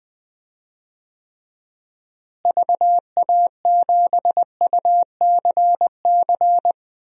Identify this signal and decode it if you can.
Morse code, heard as a keyed tone, about 20 wpm, 700 Hz. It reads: VA7UCC
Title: DXCC-100 Call Signs - 6 Characters - Repeated 2x - 20wpm